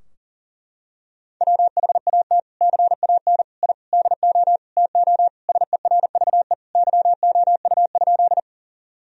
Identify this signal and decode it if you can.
Morse code, heard as a keyed tone, about 37 wpm, 700 Hz. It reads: WHAT CAN I DO TO SERVE YOU?
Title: Sentences from Top 1000 Words - Review Part 2 - 40wpm